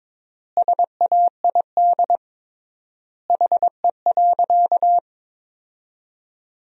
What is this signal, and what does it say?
Morse code, heard as a keyed tone, about 22 wpm, 700 Hz. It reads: SAID HE.
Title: 3x Word-Spacing - Sentences from Top 400 Words - 22wpm